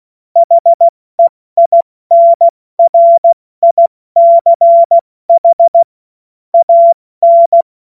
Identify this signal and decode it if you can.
Morse code, heard as a keyed tone, about 16 wpm, 680 Hz. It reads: HEINRICH AN
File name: Morsekurs / Witze / Witz 24. Januar 2023 70 F80